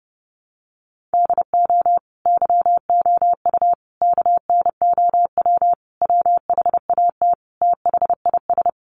Morse code, heard as a keyed tone, about 30 wpm, 700 Hz. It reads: DO YOU KNOW WHAT THIS
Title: Sentences from Top 1500 Words 30wpm